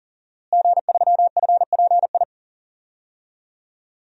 Morse code, heard as a keyed tone, about 40 wpm, 700 Hz. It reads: G3FPI
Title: DXCC-100 Call Signs - 5 Characters - Repeated 2x - 40wpm